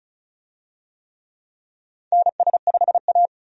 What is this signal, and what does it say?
Morse code, heard as a keyed tone, about 34 wpm, 700 Hz. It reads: NS5A
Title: USA Call Signs - 4 Characters - 35wpm